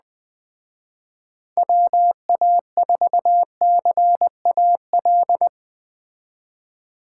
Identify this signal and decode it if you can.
Morse code, heard as a keyed tone, about 20 wpm, 700 Hz. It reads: WA4CAL
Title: Speed Racing - USA 2x3 Call Signs 17wpm